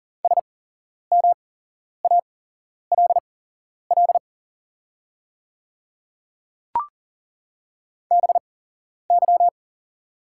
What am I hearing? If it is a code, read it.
Morse code, heard as a keyed tone, about 40 wpm, spaced as for 11 wpm, 690 Hz. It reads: SMALL BY
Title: 8x Character Spacing - Top 300 Words - 40wpm